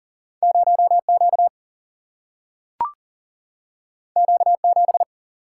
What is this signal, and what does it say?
Morse code, heard as a keyed tone, about 39 wpm, 700 Hz. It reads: 0Q Q7
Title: Single-Letter Single-Number 40wpm